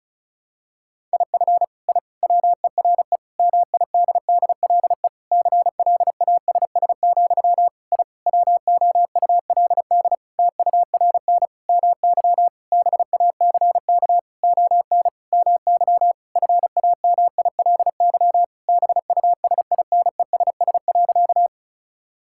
Morse code, heard as a keyed tone, about 36 wpm, 700 Hz. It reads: IF I WERE MIDDLE CLASS, I WOULD TURN MY BACK ON MY FAMILY BUSINESS.
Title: Sentences from Top 1000 Words - Review Part 6 - 35wpm